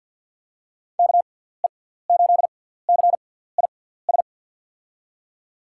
Morse code, heard as a keyed tone, about 46 wpm, spaced as for 17 wpm, 700 Hz. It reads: KE8CIS
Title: USA Call Signs - 6 Characters - 6x Character Spacing - 50wpm